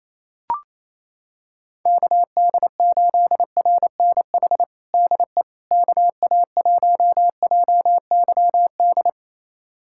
Morse code, heard as a keyed tone, about 28 wpm, 700 Hz. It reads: KD8RNH DE KA1JYD
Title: USA Callsign DE Callsign 28wpm